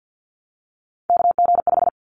Morse code, heard as a keyed tone, about 50 wpm, 700 Hz. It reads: KG5
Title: Callsign Prefixes for Top 25 DXCC - Repeated 3x - 50wpm